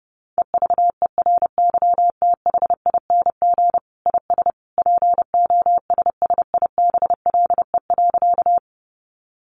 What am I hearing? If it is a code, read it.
Morse code, heard as a keyed tone, about 30 wpm, 700 Hz. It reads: EVERYTHING IS POSSIBLE.
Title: Sentences from Top 700 Words 30wpm